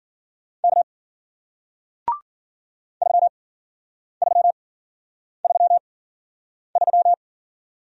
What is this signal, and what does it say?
Morse code, heard as a keyed tone, about 52 wpm, spaced as for 9 wpm, 700 Hz. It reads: K 3333